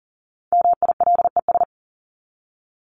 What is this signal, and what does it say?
Morse code, heard as a keyed tone, about 40 wpm, 700 Hz. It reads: MILES
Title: ES:S Suffix - Top 100 Words with Base Words 40wpm